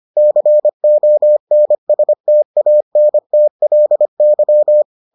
Morse code, heard as a keyed tone, about 25 wpm, 600 Hz. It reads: CONSTANTLY